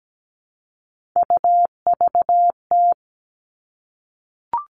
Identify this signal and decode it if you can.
Morse code, heard as a keyed tone, about 17 wpm, 700 Hz. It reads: UVT
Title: All Cut Numbers - 3-Digit 17wpm